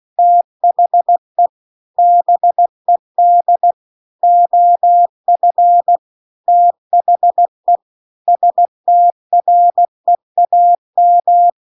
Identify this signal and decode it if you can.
Morse code, heard as a keyed tone, about 16 wpm, 700 Hz. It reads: THE BED OF THE STREAM
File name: CW-The-Big-Twoheaded-River-2-16WPM